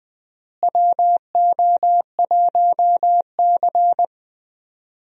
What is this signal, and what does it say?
Morse code, heard as a keyed tone, about 21 wpm, 700 Hz. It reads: WO1C
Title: USA Call Signs - 4 Characters - Repeated 3x - 20wpm